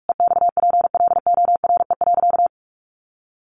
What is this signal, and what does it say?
Morse code, heard as a keyed tone, about 46 wpm, 700 Hz. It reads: EXPLORE.
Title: Sentences from Top 1100 Words 45wpm